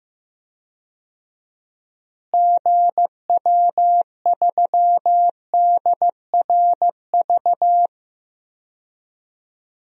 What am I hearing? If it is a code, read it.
Morse code, heard as a keyed tone, about 15 wpm, 700 Hz. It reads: GW3DRV
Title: DXCC-100 Call Signs - 6 Characters - Repeated 3x - 15wpm